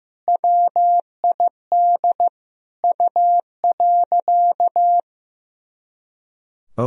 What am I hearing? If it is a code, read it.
Morse code, heard as a keyed tone, about 15 wpm, 700 Hz. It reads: WID U.
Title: Sentences from Top 500 Words and Abbreviations 15wpm